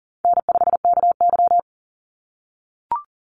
Morse code, heard as a keyed tone, about 41 wpm, 700 Hz. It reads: N5KY
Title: Easy Call Signs - 40wpm